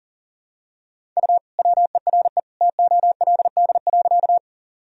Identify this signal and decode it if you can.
Morse code, heard as a keyed tone, about 40 wpm, 700 Hz. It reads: U WERE TOLD.
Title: Sentences from Top 400 Words and Abbreviations 40wpm